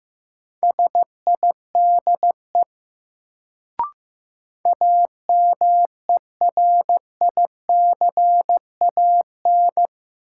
Morse code, heard as a keyed tone, about 15 wpm, 700 Hz. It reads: SIDE AMERICAN
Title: Sets of 2 Words - New Word Drill - 15wpm